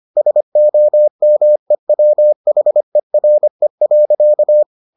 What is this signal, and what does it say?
Morse code, heard as a keyed tone, about 25 wpm, 600 Hz. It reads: SOMEWHERE.